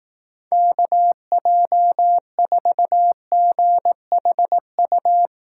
Morse code, heard as a keyed tone, about 18 wpm, 700 Hz. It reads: KJ4GHU
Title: Speed Racing - USA 2x3 Call Signs 15wpm